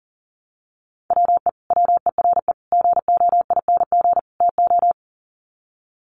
Morse code, heard as a keyed tone, about 40 wpm, 700 Hz. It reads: WE WERE GOING TO
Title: Sets of 4 Words 40wpm